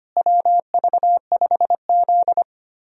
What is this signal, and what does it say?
Morse code, heard as a keyed tone, about 25 wpm, 700 Hz. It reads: WV5Z
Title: USA Call Signs - 4 Characters - Repeated 3x - 25wpm